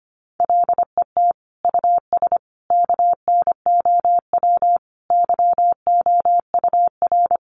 Morse code, heard as a keyed tone, about 25 wpm, 700 Hz. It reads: LET US KNOW YOUR
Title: Sentences from Top 1400 Words 25wpm